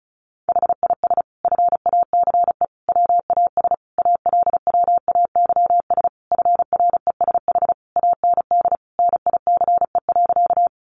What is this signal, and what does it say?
Morse code, heard as a keyed tone, about 34 wpm, 700 Hz. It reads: HIS FACE WAS ALWAYS FRESH AND NICE.
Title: Sentences from Top 1200 Words 35wpm